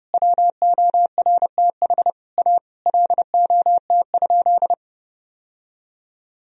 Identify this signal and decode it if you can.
Morse code, heard as a keyed tone, about 30 wpm, 700 Hz. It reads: WORTH A LOT?
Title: Sentences from Top 1000 Words - Review Part 4 - 30wpm